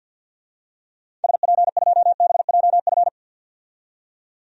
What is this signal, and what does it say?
Morse code, heard as a keyed tone, about 53 wpm, 700 Hz. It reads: SP2BJF
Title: DXCC-100 Call Signs - 6 Characters - 50wpm